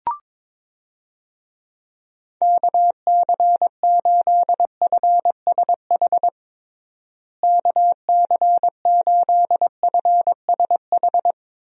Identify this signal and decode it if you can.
Morse code, heard as a keyed tone, about 21 wpm, 700 Hz. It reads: KC8FSH KC8FSH
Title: USA Call Signs - 6 Characters - Repeated 2x - 22wpm